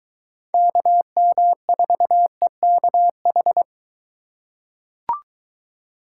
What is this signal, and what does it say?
Morse code, heard as a keyed tone, about 22 wpm, 700 Hz. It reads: KM4EKH